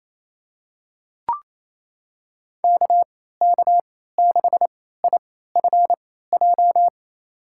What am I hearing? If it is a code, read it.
Morse code, heard as a keyed tone, about 27 wpm, spaced as for 16 wpm, 700 Hz. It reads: KK6IFJ
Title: USA Call Signs - 3x Character Spacing - 28wpm